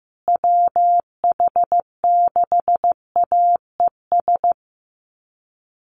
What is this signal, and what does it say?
Morse code, heard as a keyed tone, about 15 wpm, 700 Hz. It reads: WH6AES
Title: USA Call Signs Repeated 3x 15wpm